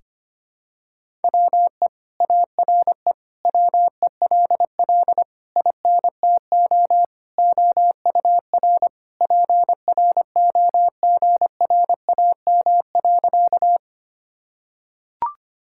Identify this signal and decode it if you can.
Morse code, heard as a keyed tone, about 25 wpm, 700 Hz. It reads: WE ARE WELL INTO OUR PROGRAM.
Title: Sentences from Top 1000 Words - Review Part 5 - 25wpm